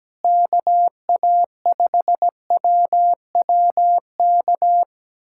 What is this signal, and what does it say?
Morse code, heard as a keyed tone, about 17 wpm, 700 Hz. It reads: KA5WWK
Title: USA Callsign DE Callsign 17wpm